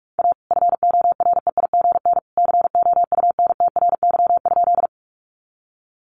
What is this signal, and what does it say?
Morse code, heard as a keyed tone, about 45 wpm, 700 Hz. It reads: A FOREIGN COUNTRY?
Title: Sentences from Top 1400 Words 45wpm